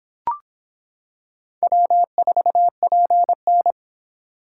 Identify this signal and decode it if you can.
Morse code, heard as a keyed tone, about 25 wpm, 700 Hz. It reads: W4PN